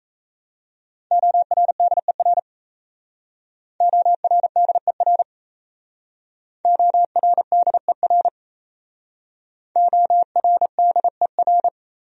Morse code, heard as a keyed tone, about 35 wpm, 700 Hz. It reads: ORDER ORDER ORDER ORDER